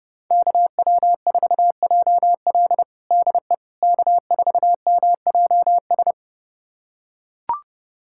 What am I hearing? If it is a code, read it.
Morse code, heard as a keyed tone, about 30 wpm, 700 Hz. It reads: KW4JL DE K4MJS
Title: USA Callsign DE Callsign - 5 Characters - 30wpm